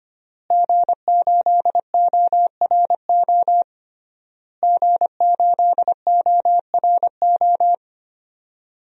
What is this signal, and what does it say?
Morse code, heard as a keyed tone, about 25 wpm, 700 Hz. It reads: G8ORO G8ORO
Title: DXCC-100 Call Signs - 5 Characters - Repeated 3x - 25wpm